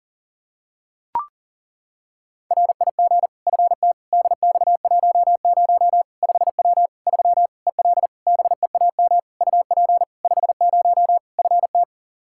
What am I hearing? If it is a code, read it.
Morse code, heard as a keyed tone, about 37 wpm, 700 Hz. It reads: RIG FT DX10 5W 3 EL BEAM UP 50 FT
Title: Practice QSO - Exchange 2 - Info - 40wpm